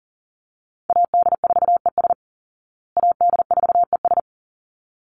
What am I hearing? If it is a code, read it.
Morse code, heard as a keyed tone, about 40 wpm, 700 Hz. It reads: AD4ES AD4ES
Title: Call Signs Repeated 3x - 40wpm